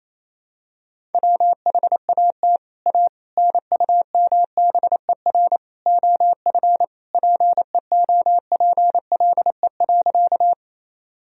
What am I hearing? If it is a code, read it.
Morse code, heard as a keyed tone, about 28 wpm, 700 Hz. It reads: WHAT A NUMBER OF PEOPLE.
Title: Sentences from Top 1000 Words - Review Part 1 - 28wpm